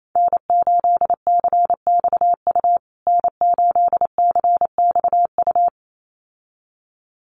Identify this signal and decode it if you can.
Morse code, heard as a keyed tone, about 28 wpm, 700 Hz. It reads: N8CXU N8CXU
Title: USA Call Signs Repeated 2x 28wpm